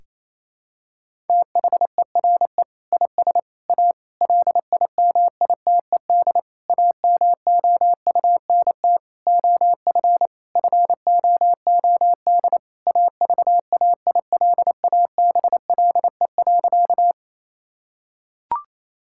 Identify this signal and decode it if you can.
Morse code, heard as a keyed tone, about 27 wpm, 700 Hz. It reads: THERE IS A LIMITED AMOUNT OF FOOD AVAILABLE.
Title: Sentences from Top 1000 Words - Review Part 6 - 28wpm